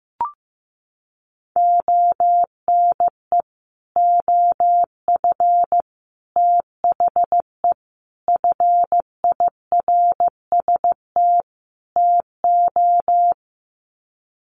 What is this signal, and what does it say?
Morse code, heard as a keyed tone, about 15 wpm, 700 Hz. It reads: ONE OF THE FIRST TO
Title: Sets of 5 Words Encore 15wpm